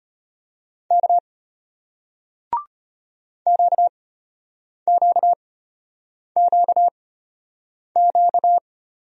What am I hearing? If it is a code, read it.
Morse code, heard as a keyed tone, about 33 wpm, spaced as for 8 wpm, 700 Hz. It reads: K QQQQ